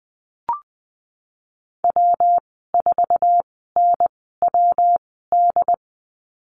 Morse code, heard as a keyed tone, about 20 wpm, spaced as for 15 wpm, 700 Hz. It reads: W4NWD
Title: 167 Lesson 27g - Number 7 - Call Signs 20wpm